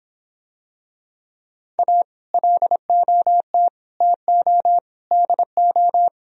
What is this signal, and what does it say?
Morse code, heard as a keyed tone, about 24 wpm, 700 Hz. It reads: A LOT TO DO